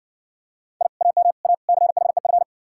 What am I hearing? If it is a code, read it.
Morse code, heard as a keyed tone, about 56 wpm, 700 Hz. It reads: I AM A CHEF